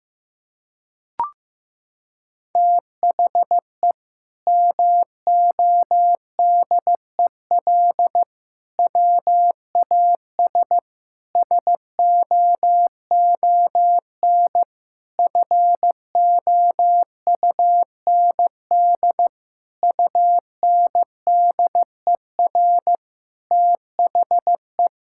Morse code, heard as a keyed tone, about 15 wpm, 700 Hz. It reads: THE MODEL WAS SOON FOUND UNDER THE
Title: Sentences from Top 1000 Words - Review Part 5 - 15wpm